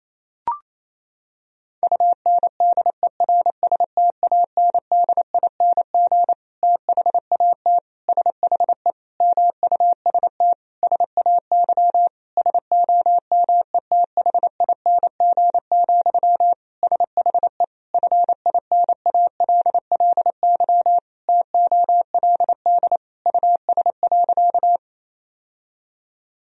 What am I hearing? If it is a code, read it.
Morse code, heard as a keyed tone, about 28 wpm, 700 Hz. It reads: UNDERSTANDING THAT SHE MUST SAY SOMETHING, SHE FINALLY TOLD US.
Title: Sentences from Top 1000 Words - Review Part 3 - 28wpm